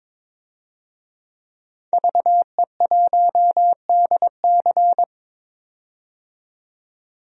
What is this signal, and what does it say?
Morse code, heard as a keyed tone, about 22 wpm, 700 Hz. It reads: VE1DC
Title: DXCC-100 Call Signs - 5 Characters - 22wpm